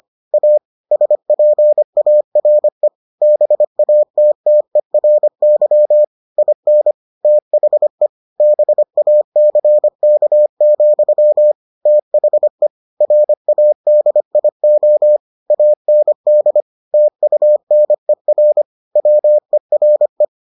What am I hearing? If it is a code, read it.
Morse code, heard as a keyed tone, about 25 wpm, 600 Hz. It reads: A SPARE BATTERY IN THE BACK, THE RADIO AND TUNER WERE